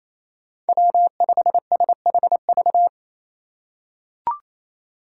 Morse code, heard as a keyed tone, about 28 wpm, 700 Hz. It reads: W5SHV
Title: USA Call Signs - 6x Character Spacing - 28wpm